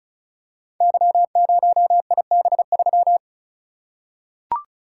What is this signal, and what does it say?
Morse code, heard as a keyed tone, about 36 wpm, 700 Hz. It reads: Y0IB3
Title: Code Groups - 5 Letter:Numbers - 35wpm